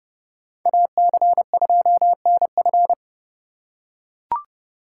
Morse code, heard as a keyed tone, about 29 wpm, 700 Hz. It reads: AC2NF